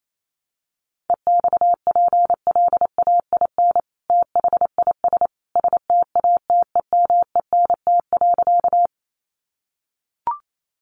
Morse code, heard as a keyed tone, about 28 wpm, 700 Hz. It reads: EXPLAIN THIS STATEMENT.
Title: Sentences from Top 1200 Words 28wpm